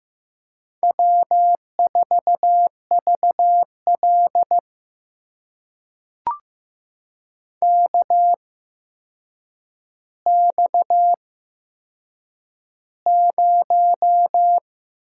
Morse code, heard as a keyed tone, about 15 wpm, spaced as for 7 wpm, 700 Hz. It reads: W4VL KX0
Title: USA Call Signs - 4 Characters - 8x Character Spacing - 15wpm